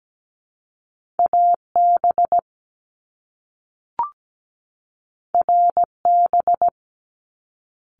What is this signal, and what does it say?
Morse code, heard as a keyed tone, about 17 wpm, spaced as for 3 wpm, 700 Hz. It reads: A BR B
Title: Top Two Letter Combinations in Words 17wpm